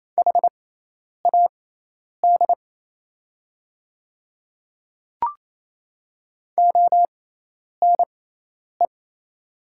Morse code, heard as a keyed tone, about 27 wpm, spaced as for 10 wpm, 700 Hz. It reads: HAD ONE
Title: 6x Character Spacing - Top 100 Words - 28wpm